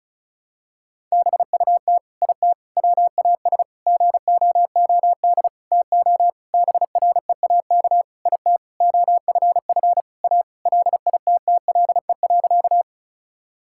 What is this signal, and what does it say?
Morse code, heard as a keyed tone, about 34 wpm, 700 Hz. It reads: BUT IT WAS GOOD TO BREAK IT OFF A LITTLE.
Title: Sentences from Top 1000 Words - Review Part 6 - 35wpm